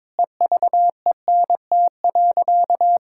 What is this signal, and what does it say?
Morse code, heard as a keyed tone, about 22 wpm, 700 Hz. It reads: EVENT.
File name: Difficult Sentences with Hints - 22wpm